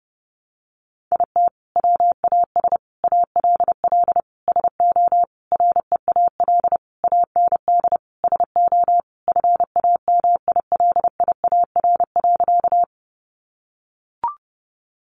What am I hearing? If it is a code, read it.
Morse code, heard as a keyed tone, about 30 wpm, 700 Hz. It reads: IT WAS ALL SO REAL AND SO FAMILIAR.
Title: Sentences from Top 1000 Words 30wpm